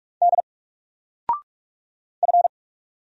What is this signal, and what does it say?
Morse code, heard as a keyed tone, about 45 wpm, 700 Hz. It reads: D F
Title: Single Letter-Number 45wpm